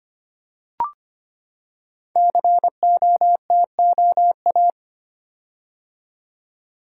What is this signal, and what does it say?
Morse code, heard as a keyed tone, about 25 wpm, 700 Hz. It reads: COTOA
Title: Code Groups - 5 Letters - 25wpm